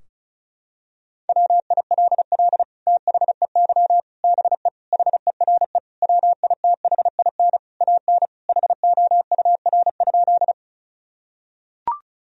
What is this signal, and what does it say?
Morse code, heard as a keyed tone, about 34 wpm, 700 Hz. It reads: WILL THEY BE HERE WITHIN AN HOUR?
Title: Sentences from Top 1000 Words - Review Part 4 - 35wpm